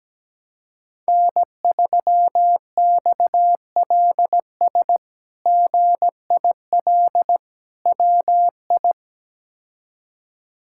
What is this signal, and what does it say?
Morse code, heard as a keyed tone, about 17 wpm, 700 Hz. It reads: N3XLS GIL WI
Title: K1USN SST QSO Practice - 17wpm